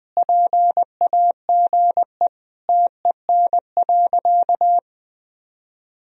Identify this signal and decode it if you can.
Morse code, heard as a keyed tone, about 20 wpm, 700 Hz. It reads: PAGE TEN.
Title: Sentences from Top 1000 Words - Review Part 2 - 20wpm